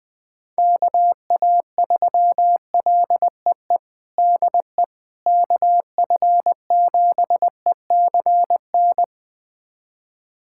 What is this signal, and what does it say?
Morse code, heard as a keyed tone, about 20 wpm, 700 Hz. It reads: KA3LEE DE KF7ECN
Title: USA Callsign DE Callsign - 6 Characters - 20wpm